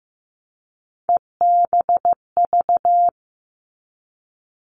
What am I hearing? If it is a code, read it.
Morse code, heard as a keyed tone, about 15 wpm, 700 Hz. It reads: EBV
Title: All Cut Numbers - 3-Digit 15wpm